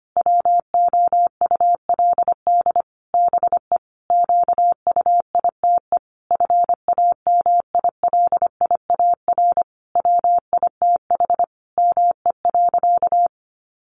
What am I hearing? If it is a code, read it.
Morse code, heard as a keyed tone, about 25 wpm, 700 Hz. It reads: WOULD BE QUITE FAMILIAR WITH ME.
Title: Sentences from Top 1000 Words 25wpm